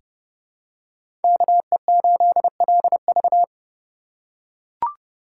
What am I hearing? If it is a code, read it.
Morse code, heard as a keyed tone, about 30 wpm, 700 Hz. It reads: KE8LV
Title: USA Call Signs - 3x Character Spacing - 30wpm